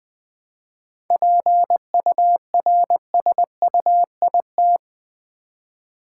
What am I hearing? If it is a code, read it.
Morse code, heard as a keyed tone, about 21 wpm, 700 Hz. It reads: PURSUIT
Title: 7 Letter Words - First 3 Letters Given - 20wpm